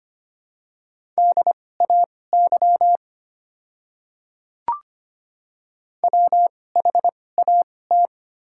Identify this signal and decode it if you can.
Morse code, heard as a keyed tone, about 25 wpm, spaced as for 18 wpm, 690 Hz. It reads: DAY WHAT
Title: 2x Character Spacing - Top 100 Words - 25wpm